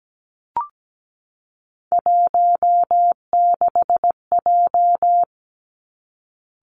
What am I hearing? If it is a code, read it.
Morse code, heard as a keyed tone, about 17 wpm, 700 Hz. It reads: 16J
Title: Three Challenging Letter-Number Combinations 17wpm